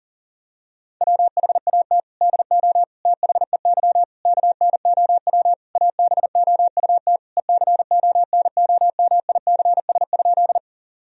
Morse code, heard as a keyed tone, about 40 wpm, 700 Hz. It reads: WHAT DO THEY KNOW ABOUT ECONOMICS?
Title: Sentences from Top 1000 Words - Review Part 3 - 40wpm